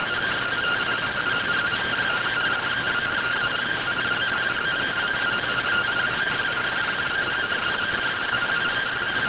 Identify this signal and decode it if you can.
Two-tone teleprinter signal, about 170 Hz shift.
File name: findings / CORONAL fr69 nb202 3mm.gen-33 rx